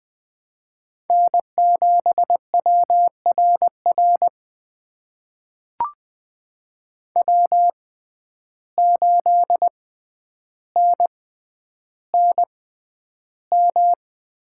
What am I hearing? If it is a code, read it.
Morse code, heard as a keyed tone, about 20 wpm, spaced as for 10 wpm, 700 Hz. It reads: N7WRR W8NNM
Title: USA Call Signs - 5 Characters - 6x Character Spacing - 20wpm